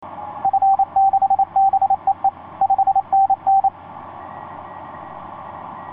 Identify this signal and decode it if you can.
Morse code, heard as a keyed tone, about 28 wpm, 780 Hz. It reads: F6BEE 5NN
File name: 6w-gm4fdm f6bee etc 15m cw 30jan2010 1502